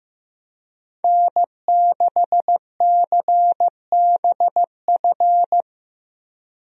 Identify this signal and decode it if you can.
Morse code, heard as a keyed tone, about 16 wpm, 700 Hz. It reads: N6CBF
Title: USA Call Signs - 5 Characters - Repeated 2x - 15wpm